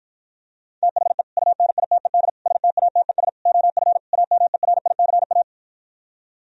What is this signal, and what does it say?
Morse code, heard as a keyed tone, about 52 wpm, 700 Hz. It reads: THE UNITED STATES OF AMERICA